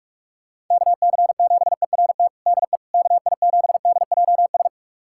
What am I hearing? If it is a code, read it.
Morse code, heard as a keyed tone, about 41 wpm, 700 Hz. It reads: KC7ERT DE KI7DJS
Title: USA Callsign DE Callsign - 6 Characters - 45wpm